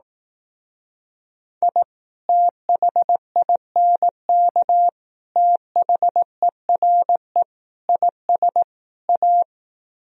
Morse code, heard as a keyed tone, about 18 wpm, 700 Hz. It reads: I THINK THERE IS A
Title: Speed-Racing - Sets of 5 Words 15wpm